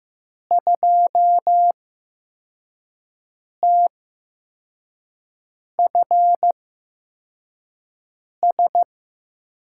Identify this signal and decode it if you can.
Morse code, heard as a keyed tone, about 15 wpm, spaced as for 4 wpm, 700 Hz. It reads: 2TFS
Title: USA Call Signs - 8x Character Spacing - 15wpm